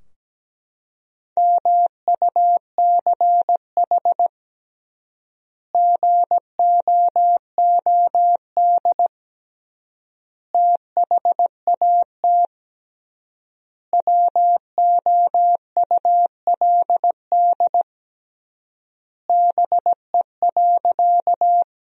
Morse code, heard as a keyed tone, about 17 wpm, 700 Hz. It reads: MUCH GOOD THAT WOULD BE.
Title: 3x Word-Spacing - Sentences from Top 200 Words - 17wpm